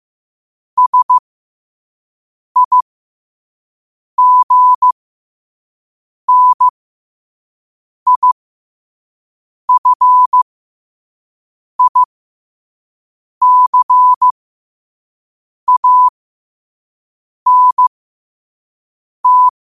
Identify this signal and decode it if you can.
Morse code, heard as a keyed tone, about 15 wpm, spaced as for 5 wpm, 1000 Hz. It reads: SIGNIFICANT
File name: qotdfiles / 5wpm